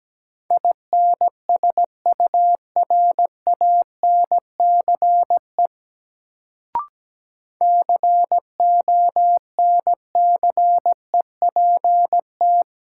Morse code, heard as a keyed tone, about 17 wpm, 700 Hz. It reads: INSURANCE CONCEPT
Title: Sentences from Top 1100 Words - New Word Drill Alt - 17wpm